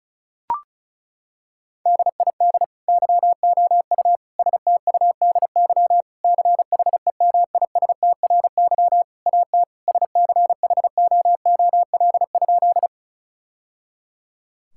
Morse code, heard as a keyed tone, about 36 wpm, 700 Hz. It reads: DID YOU STUDY CHEMISTRY AT SCHOOL?
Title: Sentences from Top 1000 Words - Review Part 6 - 35wpm